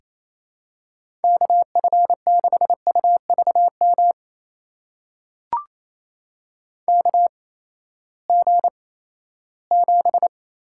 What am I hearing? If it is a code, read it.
Morse code, heard as a keyed tone, about 28 wpm, spaced as for 16 wpm, 700 Hz. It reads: KF6UVM KG7
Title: USA Call Signs - 6 Characters - 8x Character Spacing - 28wpm